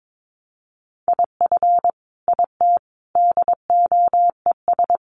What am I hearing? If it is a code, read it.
Morse code, heard as a keyed tone, about 22 wpm, 700 Hz. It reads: IF IT DOES
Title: Sentences from Top 1300 Words 22wpm